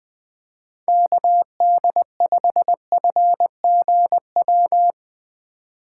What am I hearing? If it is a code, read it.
Morse code, heard as a keyed tone, about 20 wpm, 700 Hz. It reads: KD5FGW
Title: USA Call Signs - 4x Character Spacing - 20wpm